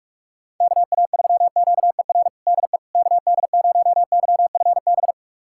Morse code, heard as a keyed tone, about 44 wpm, 700 Hz. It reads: KA3QER DE KD0YFB
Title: USA Callsign DE Callsign - 6 Characters - 45wpm